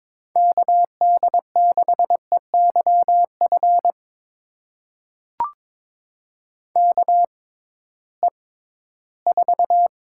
Morse code, heard as a keyed tone, about 22 wpm, spaced as for 14 wpm, 700 Hz. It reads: KD6EYF KE4
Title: USA Call Signs - 6 Characters - 6x Character Spacing - 22wpm